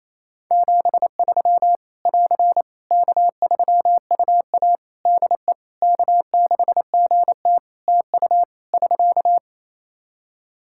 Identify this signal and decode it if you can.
Morse code, heard as a keyed tone, about 28 wpm, 700 Hz. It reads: 73 <AR> K3UA DE K6GT TU <SK>
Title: Practice QSO - Exchange 4 - Ending - 28wpm